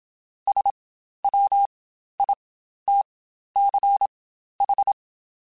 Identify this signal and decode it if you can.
Morse code, heard as a keyed tone, about 27 wpm, spaced as for 12 wpm, 800 Hz. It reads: SWITCH
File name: Spaced / Chuckles Spaced-mp3s / MiddleAge sp @25